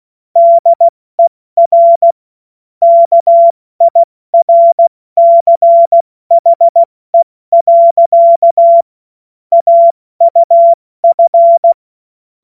Morse code, heard as a keyed tone, about 16 wpm, spaced as for 14 wpm, 680 Hz. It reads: DER KIRCHE. AUF
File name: Morsekurs / Witze / Witz 27. Juli 2021, 70 F80